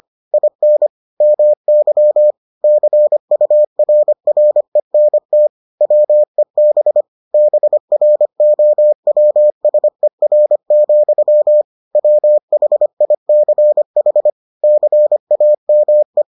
Morse code, heard as a keyed tone, about 25 wpm, 600 Hz. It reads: IN MY CURRENT WEB BROWSER, WHICH CAME